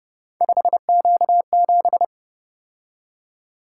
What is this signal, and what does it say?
Morse code, heard as a keyed tone, about 27 wpm, 700 Hz. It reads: 5Q7
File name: Speed-Racing - Three Letter-Number Combinations - 22wpm